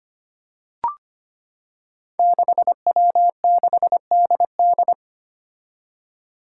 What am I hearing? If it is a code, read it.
Morse code, heard as a keyed tone, about 25 wpm, 700 Hz. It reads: 6W6DD
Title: Five Challenging Letter-Number Combinations 25wpm